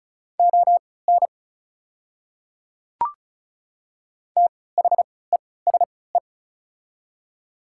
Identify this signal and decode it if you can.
Morse code, heard as a keyed tone, about 36 wpm, spaced as for 20 wpm, 690 Hz. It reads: ON THESE
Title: 3x Character Spacing - Top 100 Words - 35wpm